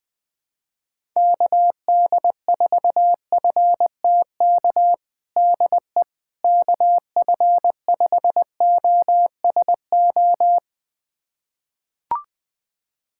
Morse code, heard as a keyed tone, about 20 wpm, 700 Hz. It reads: KD4FTK DE KF5OSO
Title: USA Callsign DE Callsign 20wpm